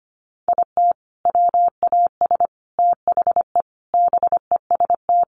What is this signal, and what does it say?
Morse code, heard as a keyed tone, about 25 wpm, 700 Hz. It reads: IT WAS THE BEST
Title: Sentences from Top 1300 Words 25wpm